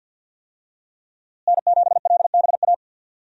Morse code, heard as a keyed tone, about 51 wpm, 700 Hz. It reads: N7LBA
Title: USA Call Signs - 5 Characters - Repeated 2x - 50wpm